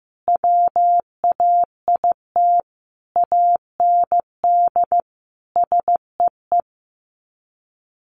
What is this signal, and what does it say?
Morse code, heard as a keyed tone, about 15 wpm, 700 Hz. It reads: WAIT AND SEE
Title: Sets of 5 Words Encore2 15wpm